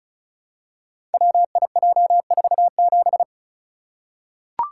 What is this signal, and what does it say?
Morse code, heard as a keyed tone, about 36 wpm, 700 Hz. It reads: WIJ47
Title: Code Groups - 5 Letter:Numbers - 35wpm